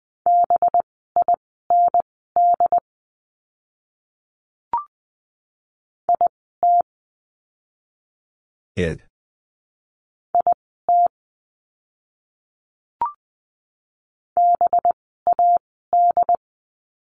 Morse code, heard as a keyed tone, about 20 wpm, spaced as for 14 wpm, 700 Hz. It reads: BIND IT IT BAD
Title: 185 Lesson 30f - Letter b - Words 20wpm